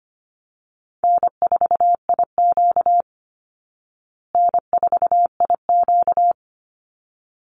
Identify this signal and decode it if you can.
Morse code, heard as a keyed tone, about 25 wpm, 700 Hz. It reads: N4IQ N4IQ
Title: Easy Call Signs Repeated 2x - 25wpm